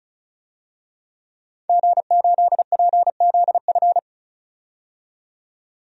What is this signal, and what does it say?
Morse code, heard as a keyed tone, about 36 wpm, 700 Hz. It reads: G8PZF
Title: DXCC-100 Call Signs - 5 Characters - 35wpm